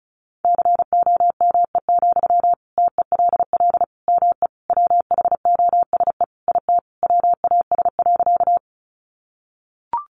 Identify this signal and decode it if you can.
Morse code, heard as a keyed tone, about 34 wpm, 700 Hz. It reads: COME, TELL ME WHOSE IT WAS.
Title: Sentences from Top 1200 Words 35wpm